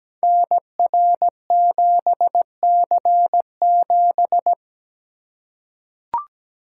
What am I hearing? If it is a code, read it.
Morse code, heard as a keyed tone, about 17 wpm, 700 Hz. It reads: NR7C7
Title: Code Groups - 5 Letter:Numbers - 17wpm